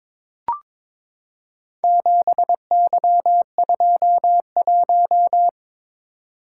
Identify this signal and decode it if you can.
Morse code, heard as a keyed tone, about 22 wpm, 700 Hz. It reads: 7Y21
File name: practice / 22wpm / Four Letter-Number Combinations - 22wpm